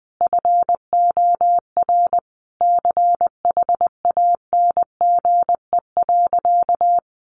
Morse code, heard as a keyed tone, about 20 wpm, 700 Hz. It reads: FOR CHANGE.
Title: Sentences from Top 1500 Words 20wpm